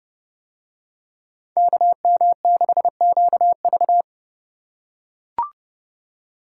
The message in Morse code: KM6QV

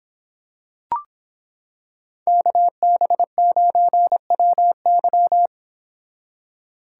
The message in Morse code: KB9WY

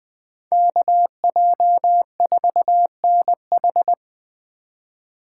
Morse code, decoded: KJ4NH

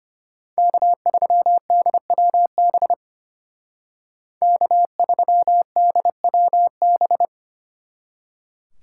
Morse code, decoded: K3DWB K3DWB

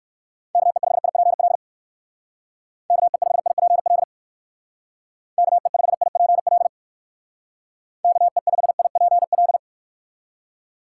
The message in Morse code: KE5IPL KE5IPL KE5IPL KE5IPL

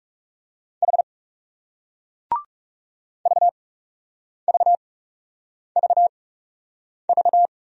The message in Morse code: H VVVV

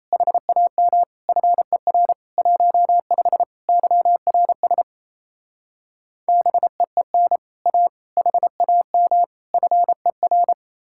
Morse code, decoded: HAM FER 15 YRS BEEN A HAM FER

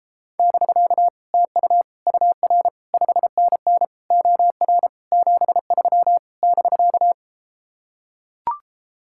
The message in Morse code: <BK> TU UR 5NN OR 73 <BK>

